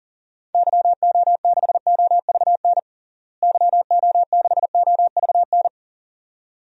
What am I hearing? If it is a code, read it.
Morse code, heard as a keyed tone, about 40 wpm, 700 Hz. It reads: YO6OVN YO6OVN
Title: DXCC-100 Call Signs - 6 Characters - Repeated 2x - 40wpm